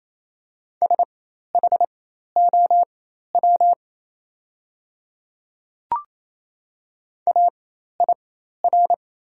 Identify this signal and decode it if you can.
Morse code, heard as a keyed tone, about 29 wpm, spaced as for 13 wpm, 690 Hz. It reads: SHOW AIR
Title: 4x Character Spacing - Top 200 Words - 28wpm